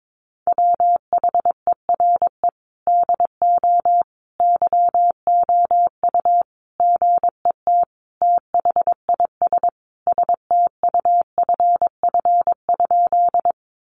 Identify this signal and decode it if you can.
Morse code, heard as a keyed tone, about 22 wpm, 700 Hz. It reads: WHERE DO YOU GET THIS STUFF?
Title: Sentences from Top 1400 Words 22wpm